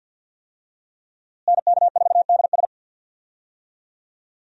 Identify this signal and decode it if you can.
Morse code, heard as a keyed tone, about 50 wpm, 700 Hz. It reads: NK4DS
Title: USA Call Signs - 5 Characters - Repeated 2x - 50wpm